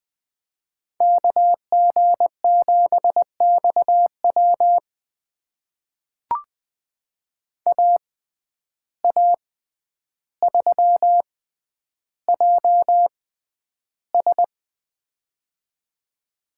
Morse code, heard as a keyed tone, about 20 wpm, spaced as for 10 wpm, 700 Hz. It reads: KG7XW AA3JS
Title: USA Call Signs - 5 Characters - 6x Character Spacing - 20wpm